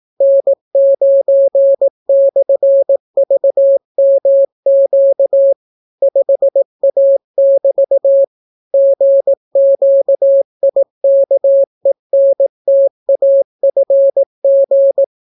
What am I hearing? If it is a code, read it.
Morse code, heard as a keyed tone, about 18 wpm, 550 Hz. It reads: N9/VMQ 5A<BT> GQIKENTAFG